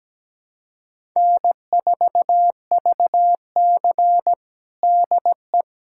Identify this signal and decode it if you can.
Morse code, heard as a keyed tone, about 17 wpm, 700 Hz. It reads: N4VC DE
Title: USA Callsign DE Callsign - 4 Characters - 17wpm